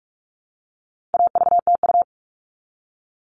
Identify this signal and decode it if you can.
Morse code, heard as a keyed tone, about 44 wpm, 700 Hz. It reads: AVTU
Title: All Cut Numbers - 4-Digit 45wpm